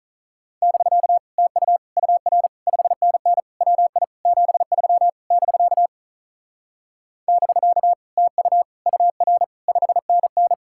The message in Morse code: <BK> TU UR 5NN WI 73 <BK> <BK> TU UR 5NN

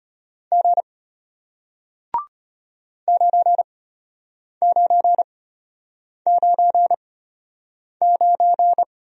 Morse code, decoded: G 9999

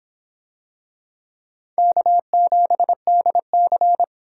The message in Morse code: K7DC

